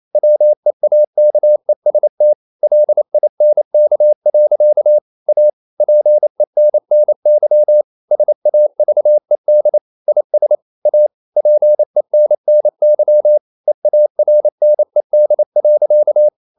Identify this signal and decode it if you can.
Morse code, heard as a keyed tone, about 28 wpm, 600 Hz. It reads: WEAKEST LINK. A PENNY SAVED IS A PENNY EARNED.
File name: CW / Practice / EnglishSayings28